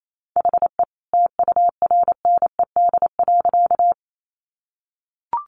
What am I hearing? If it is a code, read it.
Morse code, heard as a keyed tone, about 28 wpm, 700 Hz. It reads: HE TURNED.
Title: A Sound Of Thunder 28wpm